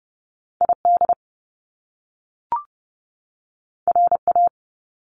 Morse code, heard as a keyed tone, about 30 wpm, 700 Hz. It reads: ID RA
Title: Top Two Letter Combinations in Words 30wpm